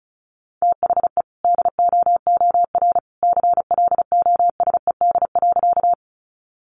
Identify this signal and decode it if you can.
Morse code, heard as a keyed tone, about 34 wpm, 700 Hz. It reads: THE DOOR CLOSED.
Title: Sentences from Top 1400 Words 35wpm